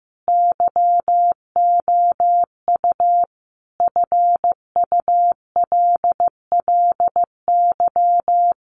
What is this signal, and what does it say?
Morse code, heard as a keyed tone, about 15 wpm, 700 Hz. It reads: YOU FULLY
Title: Sentences from Top 1300 Words 15wpm